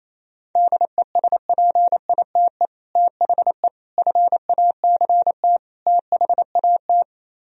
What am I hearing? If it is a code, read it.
Morse code, heard as a keyed tone, about 36 wpm, 700 Hz. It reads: DESPITE THE FACT THAT